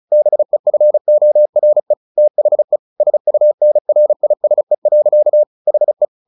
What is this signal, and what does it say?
Morse code, heard as a keyed tone, about 34 wpm, 600 Hz. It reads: BEFORE THE SUNRISE. HE
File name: the old man and the sea 2 35wpm